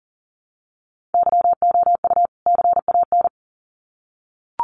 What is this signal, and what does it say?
Morse code, heard as a keyed tone, about 40 wpm, 700 Hz. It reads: YOU CAN
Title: Sets of 2 Words 40wpm